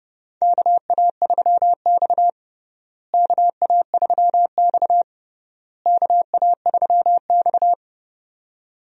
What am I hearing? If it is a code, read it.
Morse code, heard as a keyed tone, about 32 wpm, 700 Hz. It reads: KA3X KA3X KA3X
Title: USA Call Signs - 4 Characters - Repeated 3x - 30wpm